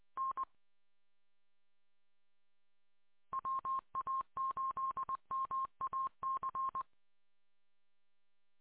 Morse code, heard as a keyed tone, about 20 wpm, 1060 Hz. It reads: N WA8MAC